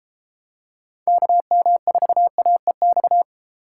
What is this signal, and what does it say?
Morse code, heard as a keyed tone, about 32 wpm, 700 Hz. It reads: KM4AEX